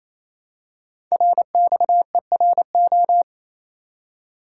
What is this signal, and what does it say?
Morse code, heard as a keyed tone, about 28 wpm, 700 Hz. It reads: RXERO
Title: Code Groups - 5 Letters - 28wpm